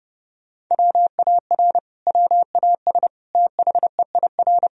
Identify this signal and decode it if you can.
Morse code, heard as a keyed tone, about 31 wpm, 700 Hz. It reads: WAR WAS THEIR
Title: Sentences from Top 1000 Words - Review Part 3 - 30wpm